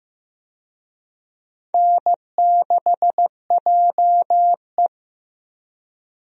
Morse code, heard as a keyed tone, about 15 wpm, 700 Hz. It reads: N6JE
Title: USA Call Signs - 4 Characters - 15wpm